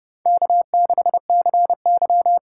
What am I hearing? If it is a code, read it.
Morse code, heard as a keyed tone, about 30 wpm, 700 Hz. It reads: K6CY
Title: USA Call Signs - 4 Characters - 2x Character Spacing - 30wpm